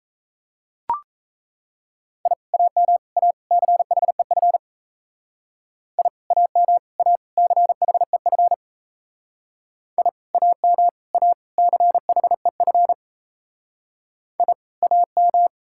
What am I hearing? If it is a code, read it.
Morse code, heard as a keyed tone, about 35 wpm, 700 Hz. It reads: I AM A CHEF I AM A CHEF I AM A CHEF I AM